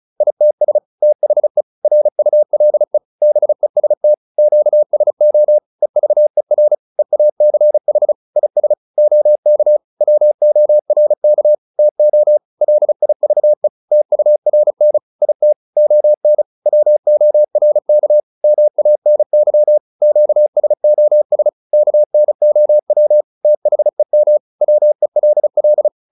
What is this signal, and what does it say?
Morse code, heard as a keyed tone, about 34 wpm, 600 Hz. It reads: ITS THE RULE BEST QSO EVER EACH IS OK WORK TO LIVE TURN IT ON WORK MANY QSOS KNOW THEM WELL